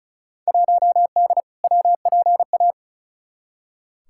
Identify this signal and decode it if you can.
Morse code, heard as a keyed tone, about 36 wpm, 700 Hz. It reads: 1D WPA
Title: Field Day Exchanges - 35wpm